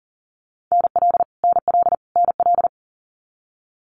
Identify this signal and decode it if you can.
Morse code, heard as a keyed tone, about 40 wpm, 700 Hz. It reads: NL NL NL
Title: Canadian Provinces and Territories - Repeated 3x - 40wpm